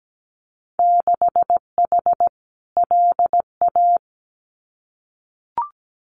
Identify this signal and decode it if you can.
Morse code, heard as a keyed tone, about 17 wpm, 700 Hz. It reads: 6H LA
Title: Winter Field Day Exchanges Encore 17wpm